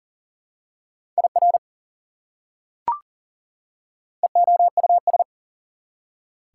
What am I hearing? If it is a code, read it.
Morse code, heard as a keyed tone, about 40 wpm, 700 Hz. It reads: IR EOUS
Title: Prefixes and Suffixes 40wpm